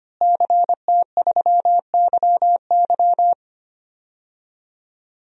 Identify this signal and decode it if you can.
Morse code, heard as a keyed tone, about 25 wpm, 700 Hz. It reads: CT3YY
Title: DXCC-100 Call Signs - 5 Characters - Repeated 3x - 25wpm